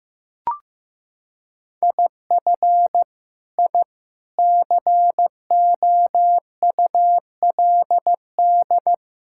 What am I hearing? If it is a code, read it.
Morse code, heard as a keyed tone, about 16 wpm, 700 Hz. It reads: IF I COULD
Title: Sentences from Top 1000 Words - Review Part 6 - 15wpm